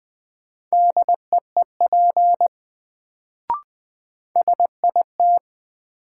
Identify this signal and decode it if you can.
Morse code, heard as a keyed tone, about 20 wpm, 700 Hz. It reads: DEEP SIT
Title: Sentences from Top 500 Words - New Word Drill - 20wpm